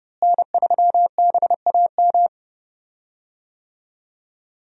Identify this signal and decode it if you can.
Morse code, heard as a keyed tone, about 30 wpm, 700 Hz. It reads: N3BAM
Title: USA Call Signs - 5 Characters - Repeated 2x - 30wpm